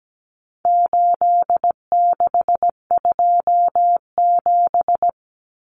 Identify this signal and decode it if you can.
Morse code, heard as a keyed tone, about 17 wpm, 700 Hz. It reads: 8627
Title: Numbers - 4-Digit 17wpm